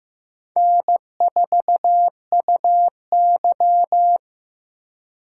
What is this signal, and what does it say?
Morse code, heard as a keyed tone, about 18 wpm, 700 Hz. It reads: N4UY
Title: Speed Racing - USA 1x2 Call Signs 15wpm